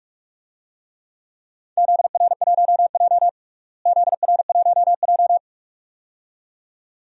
Morse code, heard as a keyed tone, about 44 wpm, 700 Hz. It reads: ZR1J ZR1J
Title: DXCC-100 Call Signs - 4 Characters - Repeated 2x - 45wpm